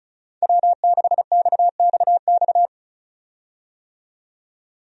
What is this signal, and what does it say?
Morse code, heard as a keyed tone, about 36 wpm, 700 Hz. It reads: W6XXX
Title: USA Call Signs - 5 Characters - 35wpm